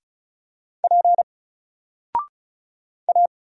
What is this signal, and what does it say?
Morse code, heard as a keyed tone, about 35 wpm, 700 Hz. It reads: P A
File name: Single-Letter-Number-35wpm